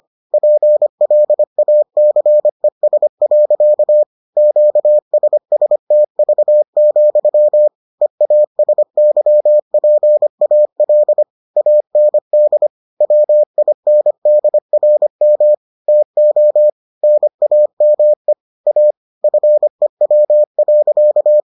Morse code, heard as a keyed tone, about 25 wpm, 600 Hz. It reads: PLACES. QSSTV, EASYPAL AND WINDRM TO NAME A FEW.